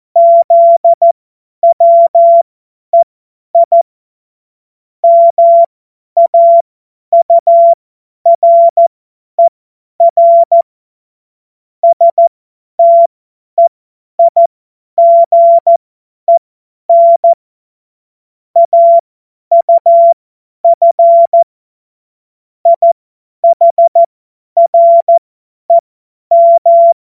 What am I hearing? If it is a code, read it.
Morse code, heard as a keyed tone, about 14 wpm, spaced as for 10 wpm, 680 Hz. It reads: ZWEI MAURER STEIGEN AUF IHREM
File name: Morsekurs / Witze / Witz 31.Oktober 2023 50 F70